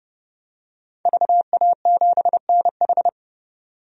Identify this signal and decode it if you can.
Morse code, heard as a keyed tone, about 30 wpm, 700 Hz. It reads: VA7NH
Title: DXCC-100 Call Signs - 5 Characters - 30wpm